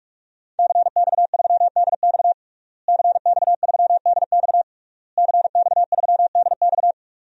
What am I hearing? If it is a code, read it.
Morse code, heard as a keyed tone, about 46 wpm, 700 Hz. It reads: CX3DX CX3DX CX3DX
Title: DXCC-100 Call Signs - 5 Characters - Repeated 3x - 45wpm